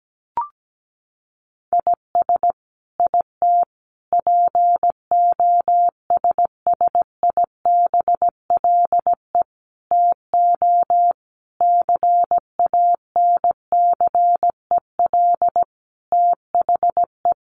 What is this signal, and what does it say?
Morse code, heard as a keyed tone, about 17 wpm, 700 Hz. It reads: IS IT POSSIBLE TO CANCEL THE
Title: Sentences from Top 1200 Words 17wpm